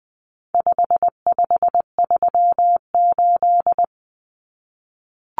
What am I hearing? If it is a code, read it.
Morse code, heard as a keyed tone, about 20 wpm, 700 Hz. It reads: 5538
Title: Common Cut Numbers - 4-Digit 20wpm